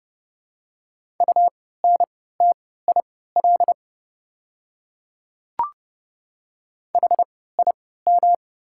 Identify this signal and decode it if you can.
Morse code, heard as a keyed tone, about 30 wpm, spaced as for 17 wpm, 700 Hz. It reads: UNTIL HIM
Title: 3x Character Spacing - Top 300 Words - 30wpm